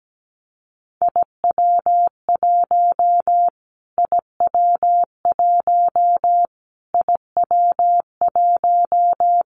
Morse code, heard as a keyed tone, about 17 wpm, 700 Hz. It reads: IW1 IW1 IW1
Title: Callsign Prefixes for Top 25 DXCC - Repeated 3x - 17wpm